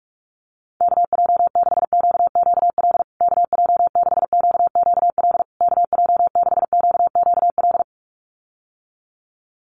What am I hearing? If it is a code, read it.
Morse code, heard as a keyed tone, about 44 wpm, 700 Hz. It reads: KJ6QQL KJ6QQL KJ6QQL
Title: USA Call Signs Repeated 3x 45wpm